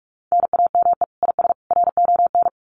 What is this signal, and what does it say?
Morse code, heard as a keyed tone, about 44 wpm, 700 Hz. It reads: NAME IS RON
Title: Common QSO Phrases 45wpm